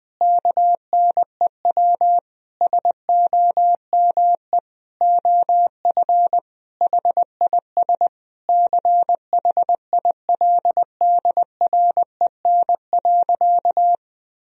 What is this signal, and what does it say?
Morse code, heard as a keyed tone, about 20 wpm, 700 Hz. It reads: KNEW SOME OF HIS CHILDREN.
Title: Sentences from Top 1000 Words - Review Part 1 - 20wpm